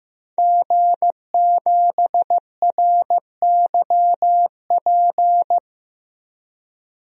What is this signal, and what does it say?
Morse code, heard as a keyed tone, about 15 wpm, 700 Hz. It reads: G7RYP
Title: DXCC-100 Call Signs - 5 Characters - 15wpm